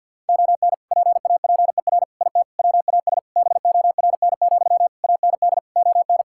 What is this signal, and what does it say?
Morse code, heard as a keyed tone, about 50 wpm, 700 Hz. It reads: ON PAPER IT WAS BORN, AND ON
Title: Sentences from Top 1600 Words 50wpm